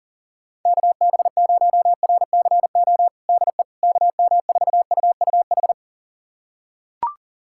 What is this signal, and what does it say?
Morse code, heard as a keyed tone, about 40 wpm, 700 Hz. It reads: KB0RCO DE KM4UUH
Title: USA Callsign DE Callsign - 6 Characters - 40wpm